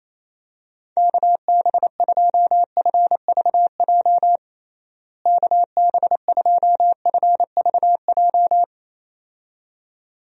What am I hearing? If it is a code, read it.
Morse code, heard as a keyed tone, about 28 wpm, 700 Hz. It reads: KB2FVJ KB2FVJ
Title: USA Call Signs - 6 Characters - Repeated 2x - 28wpm